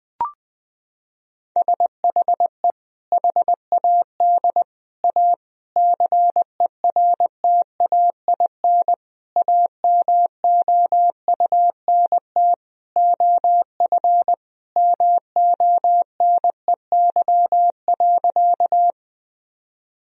Morse code, heard as a keyed tone, about 21 wpm, 700 Hz. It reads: SHE HAD A CERTAIN AMOUNT OF MONEY.
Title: Sentences from Top 1000 Words - Review Part 4 - 20wpm